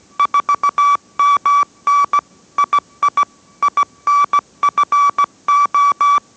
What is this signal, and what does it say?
Morse code, heard as a keyed tone, about 17 wpm, 1150 Hz. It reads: 4MN II INFO